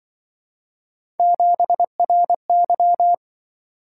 Morse code, heard as a keyed tone, about 24 wpm, 700 Hz. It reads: 7RY
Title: Speed-Racing - Three Letter-Number Combinations - 20wpm